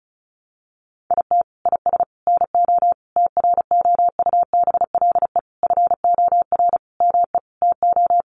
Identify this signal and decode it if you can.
Morse code, heard as a keyed tone, about 34 wpm, 700 Hz. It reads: IT IS NO TROUBLE FOR ME TO
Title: Sentences from Top 1200 Words 35wpm